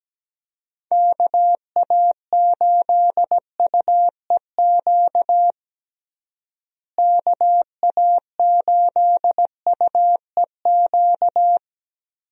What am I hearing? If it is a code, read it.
Morse code, heard as a keyed tone, about 17 wpm, 700 Hz. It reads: KA8UEQ KA8UEQ
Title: USA Call Signs - 6 Characters - Repeated 3x - 17wpm